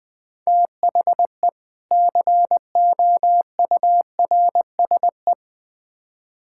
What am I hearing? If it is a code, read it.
Morse code, heard as a keyed tone, about 20 wpm, 700 Hz. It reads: THE COURSE
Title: Build A Phrase - 3-Word Phrases - 20wpm